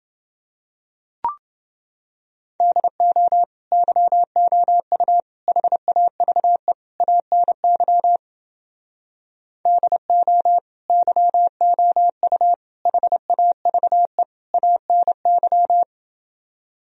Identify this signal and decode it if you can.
Morse code, heard as a keyed tone, about 30 wpm, 700 Hz. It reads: DO YOU HAVE ANY DO YOU HAVE ANY